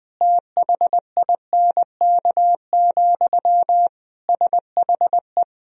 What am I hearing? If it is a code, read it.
Morse code, heard as a keyed tone, about 20 wpm, 700 Hz. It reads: THINK, SHE
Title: Sentences from Top 1000 Words - Review Part 4 - 20wpm